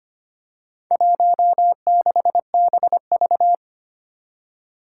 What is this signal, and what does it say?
Morse code, heard as a keyed tone, about 25 wpm, 700 Hz. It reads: 16BV
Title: Four Challenging Letter-Number Combinations 25wpm